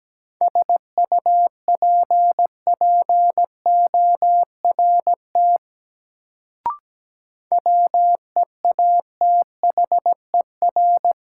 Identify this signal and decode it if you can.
Morse code, heard as a keyed tone, about 17 wpm, 700 Hz. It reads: SUPPORT WEATHER
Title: Sentences from Top 800 Words - New Word Drill - 17wpm